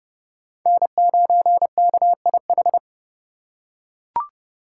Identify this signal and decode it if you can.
Morse code, heard as a keyed tone, about 32 wpm, 700 Hz. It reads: N9KIH
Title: USA Call Signs - 5 Characters - 4x Character Spacing - 30wpm